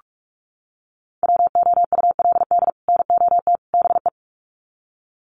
Morse code, heard as a keyed tone, about 44 wpm, 700 Hz. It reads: WOULD NOT BE
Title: Sets of 3 Words 45wpm